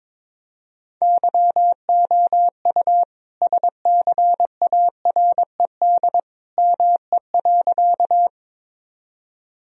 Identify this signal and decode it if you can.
Morse code, heard as a keyed tone, about 22 wpm, 700 Hz. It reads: YOU SCARED ME.
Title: Sentences from Top 1000 Words - Review Part 4 - 22wpm